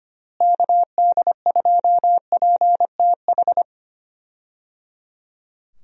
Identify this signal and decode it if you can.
Morse code, heard as a keyed tone, about 25 wpm, 700 Hz. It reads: KD2PTH
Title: USA Call Signs - 6 Characters - Repeated 3x - 25wpm